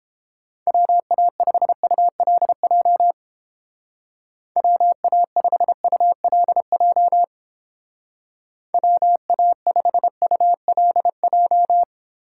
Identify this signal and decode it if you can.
Morse code, heard as a keyed tone, about 32 wpm, 700 Hz. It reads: WA5ULJ WA5ULJ WA5ULJ